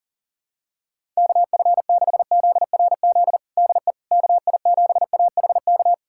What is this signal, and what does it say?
Morse code, heard as a keyed tone, about 40 wpm, 700 Hz. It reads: KF6ZRZ DE KI7AHK
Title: USA Callsign DE Callsign 40wpm